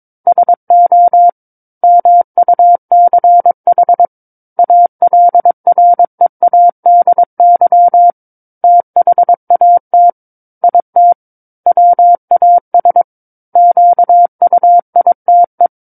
Morse code, heard as a keyed tone, about 22 wpm, 700 Hz. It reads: SO MUCH ALREADY THAT IT WAS QUITE
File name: 06-Chapter-06 22 WPM